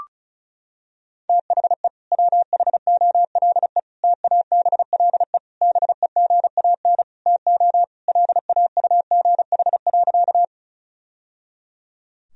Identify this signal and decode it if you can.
Morse code, heard as a keyed tone, about 34 wpm, 700 Hz. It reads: THE WHOLE TABLE BEGAN TO LAUGH.
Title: Sentences from Top 1000 Words - Review Part 2 - 35wpm